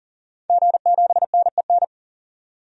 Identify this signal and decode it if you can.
Morse code, heard as a keyed tone, about 40 wpm, 700 Hz. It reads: G7NEN
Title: DXCC-100 Call Signs - 5 Characters - 40wpm